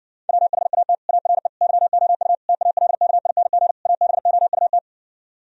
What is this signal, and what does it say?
Morse code, heard as a keyed tone, about 69 wpm, 700 Hz. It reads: WHAT ARE YOU TALKING ABOUT